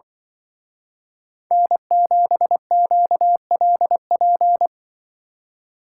Morse code, heard as a keyed tone, about 24 wpm, 700 Hz. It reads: N7QLP
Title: Speed Racing - USA 1x3 Call Signs 20wpm